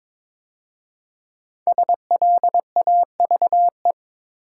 Morse code, heard as a keyed tone, about 22 wpm, 700 Hz. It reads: SLAVE
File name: 5 Letter Words - First 3 Letters Given - 22wpm